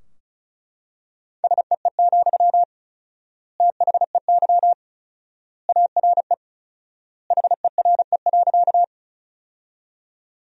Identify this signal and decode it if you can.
Morse code, heard as a keyed tone, about 34 wpm, 700 Hz. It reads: SEE, THEY ARE HERE.
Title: 4x Word-Spacing - Sentences from Top 200 Words - 35wpm